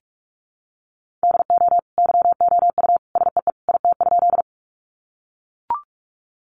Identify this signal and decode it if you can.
Morse code, heard as a keyed tone, about 44 wpm, 700 Hz. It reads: DO YOU SEE IT?
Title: Sentences from Top 100 Words 45wpm